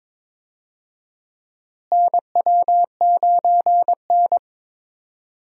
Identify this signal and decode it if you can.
Morse code, heard as a keyed tone, about 21 wpm, 700 Hz. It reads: NW9N
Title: USA Call Signs - 4 Characters - Repeated 3x - 22wpm